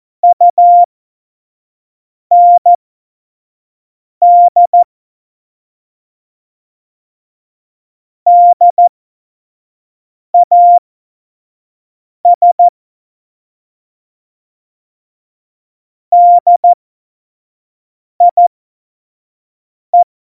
Morse code, heard as a keyed tone, about 14 wpm, spaced as for 5 wpm, 700 Hz. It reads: UND DAS DIE
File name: Morsekurs / Witze / Witz 19. September 17, Tempo 25